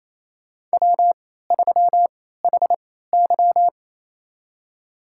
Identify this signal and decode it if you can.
Morse code, heard as a keyed tone, about 28 wpm, spaced as for 16 wpm, 700 Hz. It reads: W3HY
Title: USA Call Signs - 4 Characters - 3x Character Spacing - 28wpm